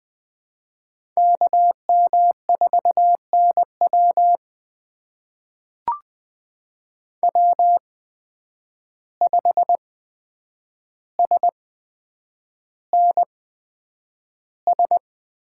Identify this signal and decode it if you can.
Morse code, heard as a keyed tone, about 20 wpm, spaced as for 9 wpm, 700 Hz. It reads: KM4NW W5SNS
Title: USA Call Signs - 5 Characters - 8x Character Spacing - 20wpm